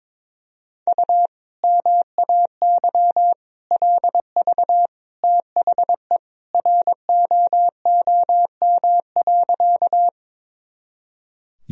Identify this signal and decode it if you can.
Morse code, heard as a keyed tone, about 22 wpm, 700 Hz. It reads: U MAY LV THE ROOM.
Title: Sentences from Top 400 Words and Abbreviations 22wpm